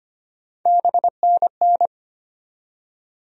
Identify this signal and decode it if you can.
Morse code, heard as a keyed tone, about 25 wpm, 700 Hz. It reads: BNN
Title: USA Call Sign Postfixes - 3 Letters 25wpm